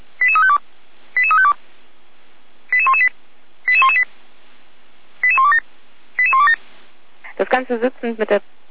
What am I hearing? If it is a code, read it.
Five-tone selective call (ZVEI), heard as quick runs of five tones, tones of about 70 ms, with speech. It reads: 88352 88352 88118 88118 88127 88127